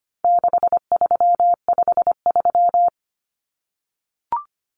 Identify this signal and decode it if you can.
Morse code, heard as a keyed tone, about 25 wpm, 700 Hz. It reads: 6353
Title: Numbers - 4-Digit 25wpm